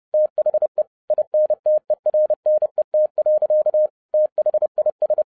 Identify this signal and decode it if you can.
Morse code, heard as a keyed tone, about 30 wpm, 610 Hz. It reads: THE INTERNET. THIS